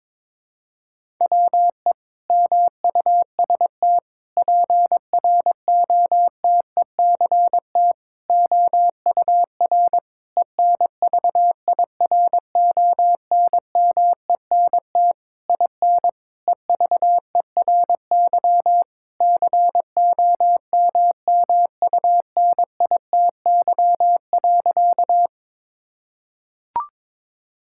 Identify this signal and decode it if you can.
Morse code, heard as a keyed tone, about 22 wpm, 700 Hz. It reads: WE MUST PROTECT OUR ENVIRONMENT IN EVERY COMMUNITY.
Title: Sentences from Top 1000 Words - Review Part 5 - 22wpm